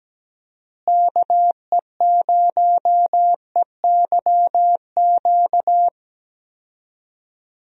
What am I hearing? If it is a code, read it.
Morse code, heard as a keyed tone, about 17 wpm, 700 Hz. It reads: KE0EYQ
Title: USA Call Signs - 6 Characters - Repeated 2x - 17wpm